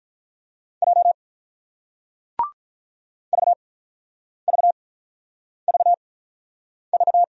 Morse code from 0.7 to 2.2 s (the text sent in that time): J